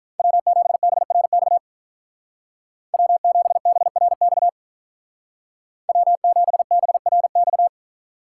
Morse code, heard as a keyed tone, about 43 wpm, 700 Hz. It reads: W7BRX W7BRX W7BRX
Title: Speed Racing - USA 1x3 Call Signs 35wpm